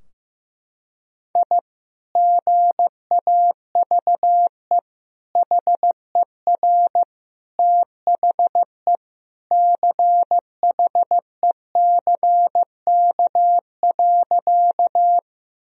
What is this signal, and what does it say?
Morse code, heard as a keyed tone, about 15 wpm, 700 Hz. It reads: I GAVE HER THE CHECK.
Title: Sentences from Top 1000 Words - Review Part 3 - 15wpm